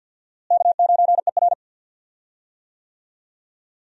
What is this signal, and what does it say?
Morse code, heard as a keyed tone, about 49 wpm, 700 Hz. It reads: K9ER
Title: USA Call Signs - 4 Characters - Repeated 3x - 50wpm